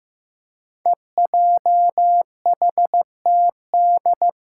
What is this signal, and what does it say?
Morse code, heard as a keyed tone, about 16 wpm, 700 Hz. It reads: EJHTD
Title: Five Challenging Letter-Number Combinations 15wpm